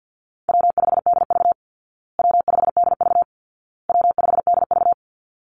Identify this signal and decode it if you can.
Morse code, heard as a keyed tone, about 50 wpm, 700 Hz. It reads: W5DV W5DV W5DV
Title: Call Signs Repeated 2x - 50wpm